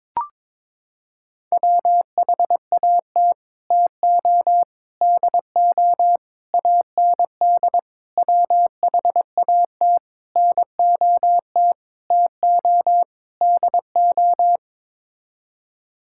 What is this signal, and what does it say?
Morse code, heard as a keyed tone, about 21 wpm, 700 Hz. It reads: WHAT TO DO AND WHAT NOT TO DO
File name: practice / 22wpm / Sets of 8 Words - 22wpm